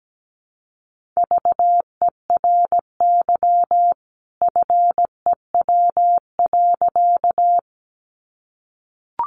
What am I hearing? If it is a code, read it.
Morse code, heard as a keyed tone, about 17 wpm, 700 Hz. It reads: VERY FEW.
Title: A Sound Of Thunder 17wpm